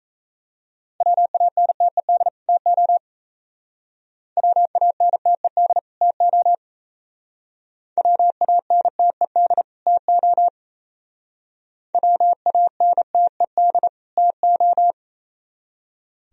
Morse code, heard as a keyed tone, about 35 wpm, 700 Hz. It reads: WANTED TO WANTED TO WANTED TO WANTED TO